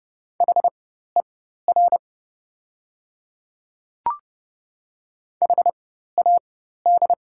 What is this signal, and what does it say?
Morse code, heard as a keyed tone, about 30 wpm, spaced as for 14 wpm, 700 Hz. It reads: HER HAD
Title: 4x Character Spacing - Top 100 Words - 30wpm